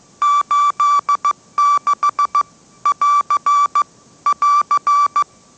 Morse code, heard as a keyed tone, about 15 wpm, 1160 Hz. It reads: 86 <AR> <AR>
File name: mrx / mrx065 16wpm